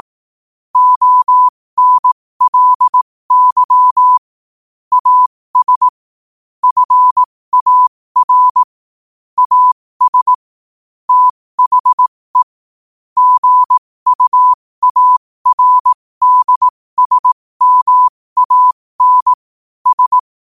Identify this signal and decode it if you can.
Morse code, heard as a keyed tone, about 18 wpm, spaced as for 15 wpm, 1000 Hz. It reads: ONLY AS FAR AS THE GUARDSMAN S
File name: qotdfiles / 15wpm